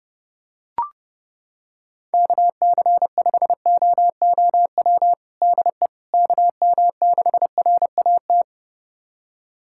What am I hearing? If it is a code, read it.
Morse code, heard as a keyed tone, about 30 wpm, 700 Hz. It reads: KC5OOW DE KM6RAT
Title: USA Callsign DE Callsign - 6 Characters - 30wpm